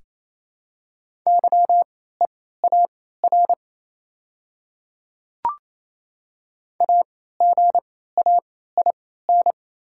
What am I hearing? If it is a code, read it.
Morse code, heard as a keyed tone, about 28 wpm, spaced as for 16 wpm, 700 Hz. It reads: YEAR AGAIN
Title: 3x Character Spacing - Top 200 Words - 28wpm